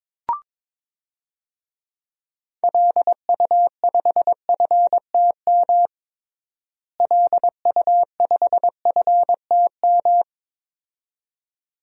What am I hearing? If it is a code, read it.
Morse code, heard as a keyed tone, about 22 wpm, 700 Hz. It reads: LU5FTM LU5FTM
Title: DXCC-100 Call Signs - 6 Characters - Repeated 2x - 22wpm